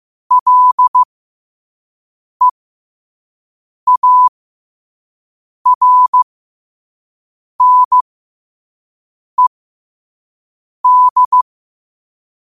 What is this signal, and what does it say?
Morse code, heard as a keyed tone, about 15 wpm, spaced as for 5 wpm, 1000 Hz. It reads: LEARNED
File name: 5wpm